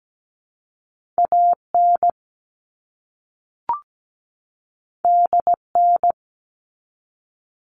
Morse code, heard as a keyed tone, about 17 wpm, 700 Hz. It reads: AN DN
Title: All Cut Numbers - 2-Digit 17wpm